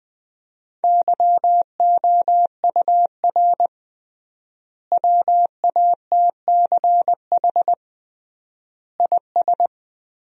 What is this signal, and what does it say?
Morse code, heard as a keyed tone, about 20 wpm, 700 Hz. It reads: YOUR WATCH IS
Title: 3x Word-Spacing - Sentences from Top 400 Words - 20wpm